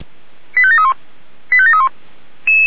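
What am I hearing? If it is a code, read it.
Five-tone selective call (ZVEI), heard as quick runs of five tones, tones of about 70 ms. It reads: 85731 85731
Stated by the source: ZVEI five-tone selective call received on SDR.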